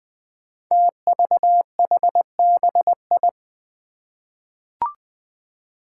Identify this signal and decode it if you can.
Morse code, heard as a keyed tone, about 21 wpm, 700 Hz. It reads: TVHBI
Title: Five Challenging Letter-Number Combinations 20wpm